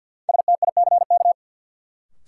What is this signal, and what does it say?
Morse code, heard as a keyed tone, about 53 wpm, 700 Hz. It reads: STICK
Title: Sentences from Top 1200 Words - New Word Drill Alt - 50wpm